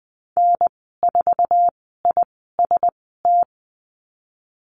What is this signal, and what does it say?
Morse code, heard as a keyed tone, about 20 wpm, spaced as for 14 wpm, 700 Hz. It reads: N4IST
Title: 062 Lesson 11g - Letter h - Call Signs 20wpm